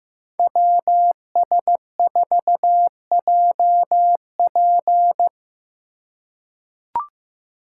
Morse code, heard as a keyed tone, about 15 wpm, 700 Hz. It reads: WS4JP
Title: USA Call Signs - 5 Characters - 3x Character Spacing - 15wpm